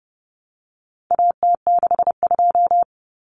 Morse code, heard as a keyed tone, about 30 wpm, 700 Hz. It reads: AT62
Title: Common Cut Numbers - 4-Digit 30wpm